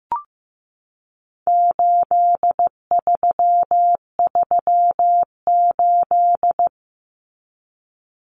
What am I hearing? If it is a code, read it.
Morse code, heard as a keyed tone, about 15 wpm, 700 Hz. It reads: 8338
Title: Numbers - 4-Digit 15wpm